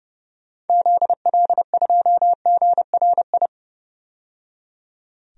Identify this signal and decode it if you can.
Morse code, heard as a keyed tone, about 30 wpm, 700 Hz. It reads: ZL2GRI
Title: DXCC-100 Call Signs - 6 Characters - 30wpm